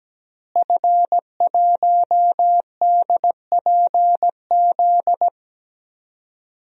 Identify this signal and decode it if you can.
Morse code, heard as a keyed tone, about 17 wpm, 700 Hz. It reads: F1DPZ
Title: DXCC-100 Call Signs - 5 Characters - Repeated 3x - 17wpm